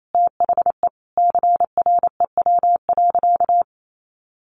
Morse code, heard as a keyed tone, about 28 wpm, 700 Hz. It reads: THE CREW.
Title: Sentences from Top 1500 Words 28wpm